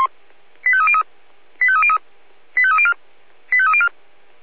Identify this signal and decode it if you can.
Five-tone selective call (ZVEI), heard as quick runs of five tones, tones of about 70 ms. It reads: 85303 85303 85304 85304